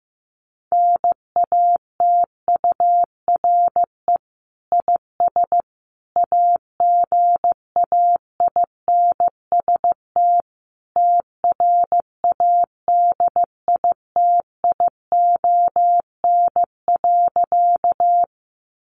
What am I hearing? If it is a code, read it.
Morse code, heard as a keyed tone, about 15 wpm, 700 Hz. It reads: NATURE IS AGAINST TRADITION.
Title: Sentences from Top 1500 Words 15wpm